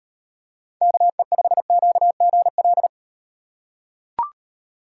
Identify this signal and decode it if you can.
Morse code, heard as a keyed tone, about 38 wpm, 700 Hz. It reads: KE5QGL